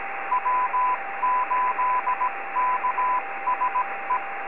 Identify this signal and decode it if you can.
Morse code, heard as a keyed tone, about 17 wpm, 1000 Hz, with radio noise. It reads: W8KSE
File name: 80m-snip